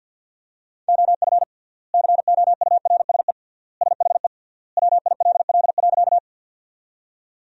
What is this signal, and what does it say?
Morse code, heard as a keyed tone, about 50 wpm, 700 Hz. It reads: OF COURSE SHE WILL.
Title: 3x Word-Spacing - Sentences from Top 500 Words - 50wpm